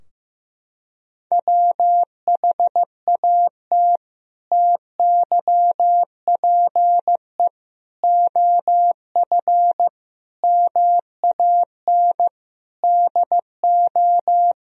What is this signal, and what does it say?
Morse code, heard as a keyed tone, about 15 wpm, 700 Hz. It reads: WHAT TYPE OF MAN DO
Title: Sentences from Top 1000 Words - Review Part 3 - 15wpm